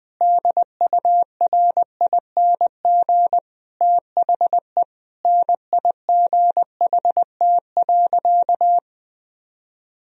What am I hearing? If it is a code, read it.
Morse code, heard as a keyed tone, about 20 wpm, 700 Hz. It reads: DURING THE NIGHT.
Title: Sentences from Top 1000 Words - Review Part 4 - 20wpm